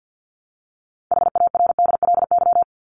Morse code, heard as a keyed tone, about 50 wpm, 700 Hz. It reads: HARDLY
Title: LY Suffix - Top 100 Words with Base Words 50wpm